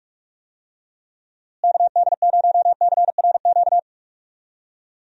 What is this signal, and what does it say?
Morse code, heard as a keyed tone, about 41 wpm, 700 Hz. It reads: KD0CRQ
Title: USA Call Signs - 6 Characters - Repeated 3x - 45wpm